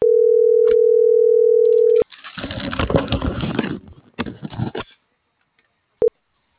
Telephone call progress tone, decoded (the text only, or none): ringback tone